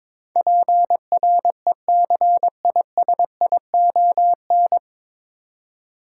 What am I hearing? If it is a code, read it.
Morse code, heard as a keyed tone, about 22 wpm, 700 Hz. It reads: PRECISION
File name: 9 Letter Words - First 3 Letters Given - 22wpm